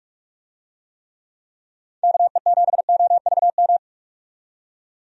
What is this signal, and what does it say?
Morse code, heard as a keyed tone, about 46 wpm, 700 Hz. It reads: KE7OVM
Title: USA Call Signs - 6 Characters - 45wpm